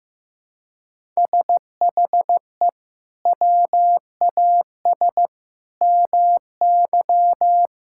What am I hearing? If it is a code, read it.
Morse code, heard as a keyed tone, about 15 wpm, 700 Hz. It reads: SHE WAS MY
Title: Sentences from Top 1000 Words - Review Part 5 - 15wpm